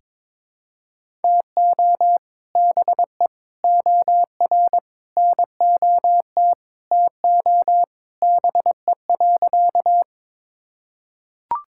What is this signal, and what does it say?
Morse code, heard as a keyed tone, about 22 wpm, 700 Hz. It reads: TO BE OR NOT TO BE.
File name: practice / 22wpm / Sentences with 2-Word Phrases - 22wpm